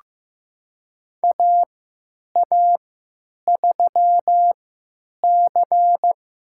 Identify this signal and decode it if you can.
Morse code, heard as a keyed tone, about 15 wpm, spaced as for 9 wpm, 700 Hz. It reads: AA3C
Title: USA Call Signs - 3x Character Spacing - 15wpm